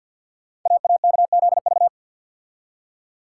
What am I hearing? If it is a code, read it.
Morse code, heard as a keyed tone, about 51 wpm, 700 Hz. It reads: AAKZV
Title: Code Groups - 5 Letter:Numbers - 50wpm